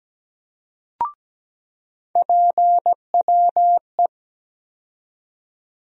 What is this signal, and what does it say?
Morse code, heard as a keyed tone, about 17 wpm, 700 Hz. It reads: PWE
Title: USA Call Sign Postfixes - 3 Letters 17wpm